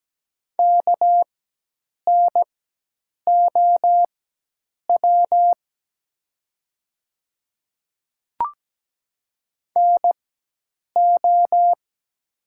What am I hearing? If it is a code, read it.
Morse code, heard as a keyed tone, about 17 wpm, spaced as for 8 wpm, 700 Hz. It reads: KNOW NO
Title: 4x Character Spacing - Top 100 Words - 17wpm